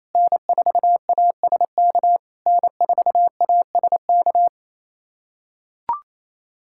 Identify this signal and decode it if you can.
Morse code, heard as a keyed tone, about 27 wpm, 690 Hz. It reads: N4ASK N4ASK
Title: SOTA QSO Practice 28wpm